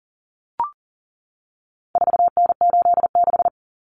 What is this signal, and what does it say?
Morse code, heard as a keyed tone, about 40 wpm, 700 Hz. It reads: 4N86
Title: Common Cut Numbers - 4-Digit 40wpm